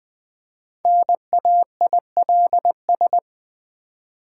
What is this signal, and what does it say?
Morse code, heard as a keyed tone, about 20 wpm, 700 Hz. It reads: NAILS
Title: Easy Words with Hints - 20wpm